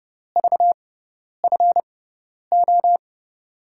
Morse code, heard as a keyed tone, about 30 wpm, spaced as for 10 wpm, 700 Hz. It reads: VFO
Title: USA Call Signs - 5 Characters - 6x Character Spacing - 30wpm